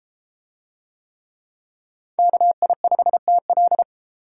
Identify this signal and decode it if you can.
Morse code, heard as a keyed tone, about 32 wpm, 700 Hz. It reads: KI5TL